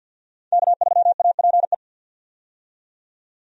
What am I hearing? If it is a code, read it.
Morse code, heard as a keyed tone, about 52 wpm, 700 Hz. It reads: K3APE
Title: USA Call Signs - 5 Characters - 50wpm